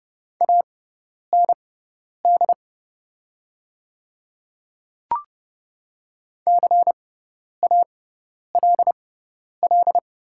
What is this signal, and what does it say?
Morse code, heard as a keyed tone, about 30 wpm, spaced as for 10 wpm, 700 Hz. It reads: AND CALL
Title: 6x Character Spacing - Top 100 Words - 30wpm